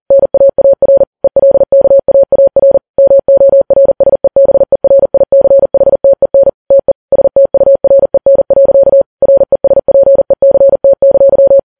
Morse code, heard as a keyed tone, about 40 wpm, 550 Hz. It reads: NAAR ELKAAR MORSEBERICHTEN TE STUREN. RESPECT!
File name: Files Web tls / Morse on9fv / morse40kps